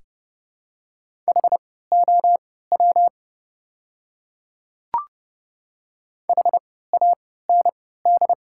Morse code, heard as a keyed tone, about 30 wpm, spaced as for 17 wpm, 700 Hz. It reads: HOW HAND
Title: 3x Character Spacing - Top 200 Words - 30wpm